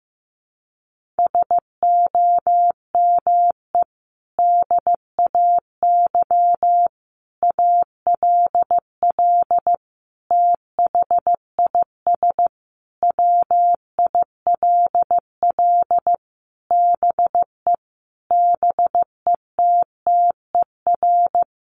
Morse code, heard as a keyed tone, about 15 wpm, 700 Hz. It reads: SOME DAY ALL THIS WILL BE BETTER
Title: Sentences from Top 1300 Words 15wpm